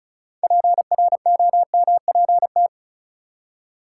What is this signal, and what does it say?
Morse code, heard as a keyed tone, about 36 wpm, 700 Hz. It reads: PROMPT
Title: Six Letter Words - 35wpm